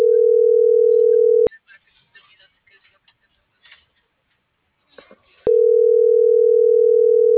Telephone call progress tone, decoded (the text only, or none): ringback tone